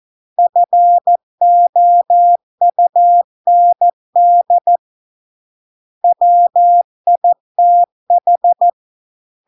Morse code, heard as a keyed tone, about 14 wpm, 700 Hz. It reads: FOUND WITH